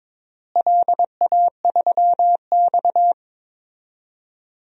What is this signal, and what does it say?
Morse code, heard as a keyed tone, about 22 wpm, 700 Hz. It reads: LA3X
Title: DXCC-100 Call Signs - 4 Characters - Repeated 2x - 22wpm